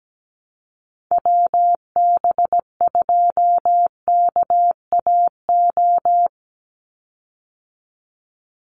Morse code, heard as a keyed tone, about 17 wpm, 700 Hz. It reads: WB2KAO
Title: Call Signs Repeated 3x - 17wpm